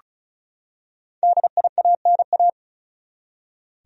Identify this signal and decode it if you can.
Morse code, heard as a keyed tone, about 34 wpm, 700 Hz. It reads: DIANA
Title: USA Common Names - 35wpm